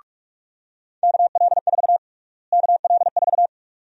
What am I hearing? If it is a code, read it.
Morse code, heard as a keyed tone, about 46 wpm, 700 Hz. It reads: KL4 KL4
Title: USA Call Sign Prefixes - Repeated 2x - 45wpm